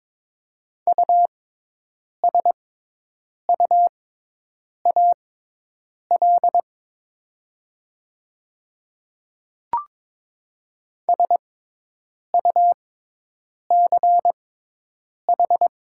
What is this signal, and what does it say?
Morse code, heard as a keyed tone, about 22 wpm, spaced as for 8 wpm, 700 Hz. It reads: USUAL SUCH
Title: 6x Character Spacing - Top 500 Words - 22wpm